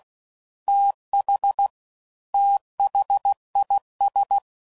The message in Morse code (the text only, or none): TH THIS